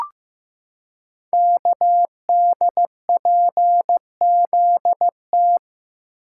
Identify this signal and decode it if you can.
Morse code, heard as a keyed tone, about 15 wpm, 700 Hz. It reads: KDPZT
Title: Code Groups - 5 Letters - 15wpm